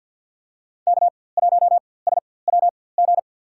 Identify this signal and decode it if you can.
Morse code, heard as a keyed tone, about 53 wpm, spaced as for 24 wpm, 700 Hz. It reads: K1SWG
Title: USA Call Signs - 5 Characters - 4x Character Spacing - 50wpm